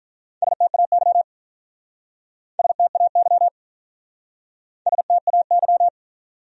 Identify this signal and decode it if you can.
Morse code, heard as a keyed tone, about 44 wpm, 700 Hz. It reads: STAY STAY STAY
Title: Speed-Racing Top 500 Words 35wpm